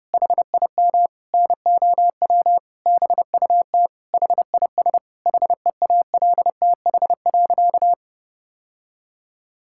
Morse code, heard as a keyed tone, about 32 wpm, 700 Hz. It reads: HIM NOW BUT HIS HEALTH.
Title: Sentences from Top 1000 Words - Review Part 6 - 30wpm